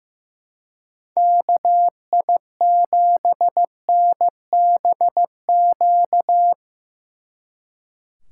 Morse code, heard as a keyed tone, about 17 wpm, 700 Hz. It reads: KI7NBQ
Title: Speed Racing - USA 2x3 Call Signs 15wpm